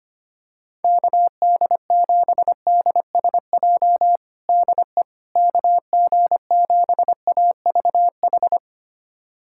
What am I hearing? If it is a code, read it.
Morse code, heard as a keyed tone, about 25 wpm, 700 Hz. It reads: KD7DSJ DE KG7AVH
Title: USA Callsign DE Callsign 25wpm